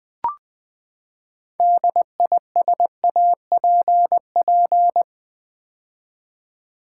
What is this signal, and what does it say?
Morse code, heard as a keyed tone, about 20 wpm, 700 Hz. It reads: DISAPP
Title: Build A Word - 6 Letters At A Time - 12-Letter Words - 20wpm